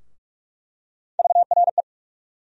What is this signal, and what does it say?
Morse code, heard as a keyed tone, about 42 wpm, 700 Hz. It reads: VRE
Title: USA Call Sign Postfixes - 3 Letters 45wpm